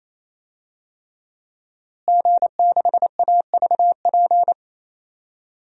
Morse code, heard as a keyed tone, about 28 wpm, 700 Hz. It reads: G6AVP
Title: DXCC-100 Call Signs - 5 Characters - Repeated 3x - 28wpm